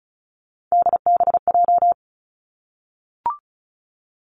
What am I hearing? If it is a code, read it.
Morse code, heard as a keyed tone, about 36 wpm, 700 Hz. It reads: DBJ
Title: Three Challenging Letter-Number Combinations 35wpm